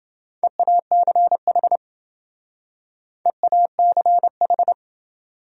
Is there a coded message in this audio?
Yes: Morse code from 0.4 to 4.7 s, heard as a keyed tone, about 27 wpm, 700 Hz. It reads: EACH EACH